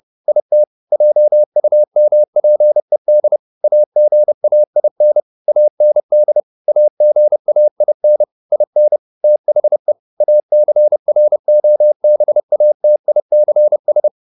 Morse code, heard as a keyed tone, about 30 wpm, 600 Hz. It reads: IT JUMPED AGAIN AND AGAIN IN THE ACROBATICS
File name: the old man and the sea 2 30wpm